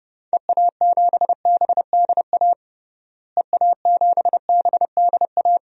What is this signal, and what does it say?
Morse code, heard as a keyed tone, about 32 wpm, 700 Hz. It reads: EA7BDA EA7BDA
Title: DXCC-100 Call Signs - 6 Characters - Repeated 2x - 30wpm